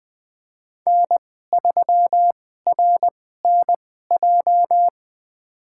Morse code, heard as a keyed tone, about 21 wpm, spaced as for 15 wpm, 700 Hz. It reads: N3RNJ
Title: USA Call Signs - 5 Characters - 2x Character Spacing - 20wpm